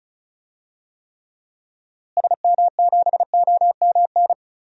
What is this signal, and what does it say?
Morse code, heard as a keyed tone, about 34 wpm, 700 Hz. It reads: SM7OMN
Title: DXCC-100 Call Signs - 6 Characters - Repeated 3x - 35wpm